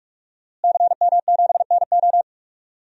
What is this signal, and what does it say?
Morse code, heard as a keyed tone, about 45 wpm, 700 Hz. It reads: CM7NO
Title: DXCC-100 Call Signs - 5 Characters - Repeated 3x - 45wpm